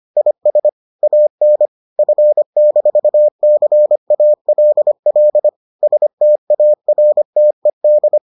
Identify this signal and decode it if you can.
Morse code, heard as a keyed tone, about 25 wpm, 600 Hz. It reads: IS AN F-CALL STARTED